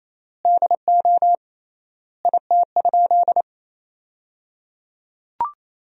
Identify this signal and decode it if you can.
Morse code, heard as a keyed tone, about 27 wpm, 700 Hz. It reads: DO IT?
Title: 3x Word-Spacing - Sentences from Top 100 Words - 28wpm